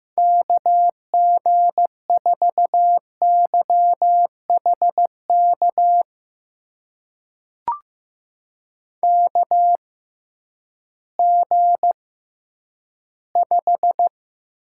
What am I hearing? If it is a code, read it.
Morse code, heard as a keyed tone, about 15 wpm, spaced as for 10 wpm, 700 Hz. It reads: KG4YHK KG5
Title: USA Call Signs - 6 Characters - 6x Character Spacing - 15wpm